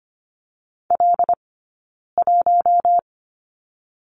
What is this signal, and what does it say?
Morse code, heard as a keyed tone, about 24 wpm, 700 Hz. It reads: L 1
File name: course / 25:10wpm / 092 Lesson 16d - Letter c - Two Characters 25wpm